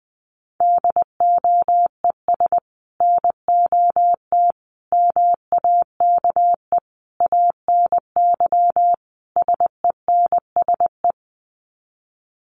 Morse code, heard as a keyed tone, about 20 wpm, 700 Hz. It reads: DOES NOT MAKE ANY SENSE
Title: Sets of 5 Words Encore2 20wpm